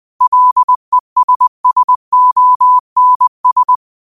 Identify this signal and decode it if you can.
Morse code, heard as a keyed tone, about 20 wpm, 1000 Hz. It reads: LESSONS
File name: qotdfiles / 20wpm